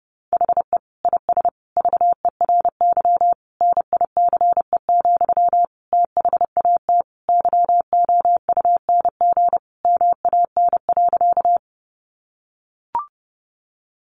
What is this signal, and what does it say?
Morse code, heard as a keyed tone, about 30 wpm, 700 Hz. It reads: HE IS VERY NICE, THAT YOUNG MAN.
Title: Sentences from Top 1100 Words 30wpm